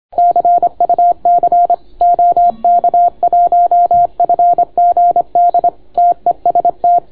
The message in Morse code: CUC OK1FGD TEST

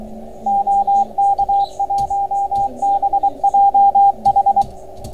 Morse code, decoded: OK1BJH

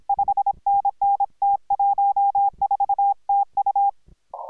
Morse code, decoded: 5NN T14 TU

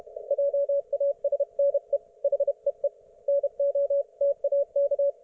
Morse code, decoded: JASNE HEE NOTAK

